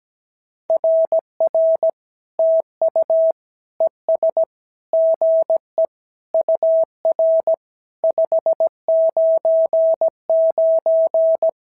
Morse code, heard as a keyed tone, about 16 wpm, 650 Hz. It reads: RR TU ES GE UR 599
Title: SOTA QSO Practice 17wpm